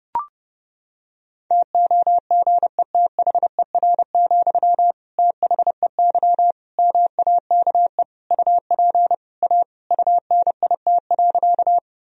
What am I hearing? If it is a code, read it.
Morse code, heard as a keyed tone, about 32 wpm, 700 Hz. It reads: TOGETHER, THEY MAKE UP A UNIT.
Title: Sentences from Top 1000 Words - Review Part 2 - 30wpm